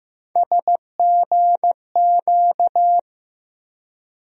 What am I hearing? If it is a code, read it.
Morse code, heard as a keyed tone, about 15 wpm, 700 Hz. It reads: SGQ
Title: USA Call Sign Postfixes - 3 Letters 15wpm